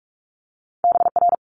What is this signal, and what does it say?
Morse code, heard as a keyed tone, about 46 wpm, 700 Hz. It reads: BR
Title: Multi-letter Phonemes Quick Fire 45wpm